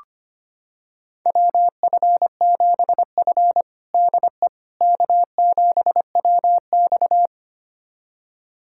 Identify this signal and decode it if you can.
Morse code, heard as a keyed tone, about 25 wpm, 700 Hz. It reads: WF7F DE K7WX
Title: USA Callsign DE Callsign - 4 Characters - 25wpm